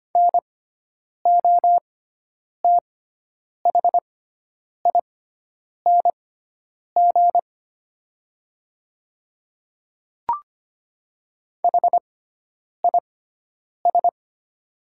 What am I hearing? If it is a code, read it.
Morse code, heard as a keyed tone, about 25 wpm, spaced as for 9 wpm, 690 Hz. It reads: NOTHING HIS
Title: 6x Character Spacing - Top 500 Words - 25wpm